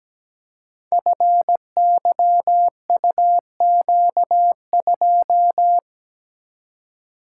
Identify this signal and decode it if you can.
Morse code, heard as a keyed tone, about 17 wpm, 700 Hz. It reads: FYUQ2
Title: Code Groups - 5 Letter:Numbers - 17wpm